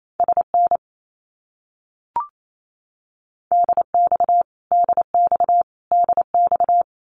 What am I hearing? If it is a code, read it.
Morse code, heard as a keyed tone, about 28 wpm, 700 Hz. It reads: SN DX DX DX
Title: Abbreviations Repeated 3x - 28wpm